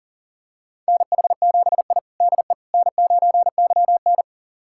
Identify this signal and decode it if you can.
Morse code, heard as a keyed tone, about 40 wpm, 700 Hz. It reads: NH7I DE N9YN
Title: USA Callsign DE Callsign - 4 Characters - 40wpm